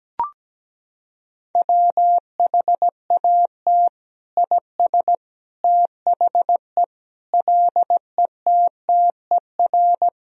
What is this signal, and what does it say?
Morse code, heard as a keyed tone, about 17 wpm, 700 Hz. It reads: WHAT IS THE LETTER
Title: Sentences from Top 1000 Words - Review Part 2 - 17wpm